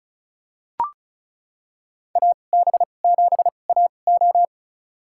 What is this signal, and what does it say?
Morse code, heard as a keyed tone, about 36 wpm, spaced as for 25 wpm, 700 Hz. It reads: AB7AO
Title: USA Call Signs - 5 Characters - 2x Character Spacing - 35wpm